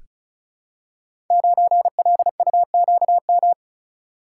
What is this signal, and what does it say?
Morse code, heard as a keyed tone, about 36 wpm, 700 Hz. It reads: 9LUQM
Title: Code Groups - 5 Letter:Numbers - 35wpm